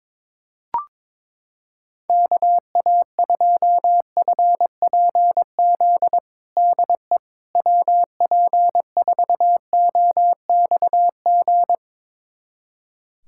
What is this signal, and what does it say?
Morse code, heard as a keyed tone, about 22 wpm, 700 Hz. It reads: KA2FPZ DE WP4OXG
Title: USA Callsign DE Callsign - 6 Characters - 22wpm